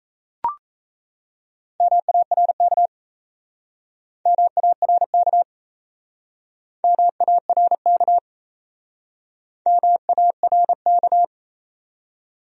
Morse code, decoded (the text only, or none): MARK MARK MARK MARK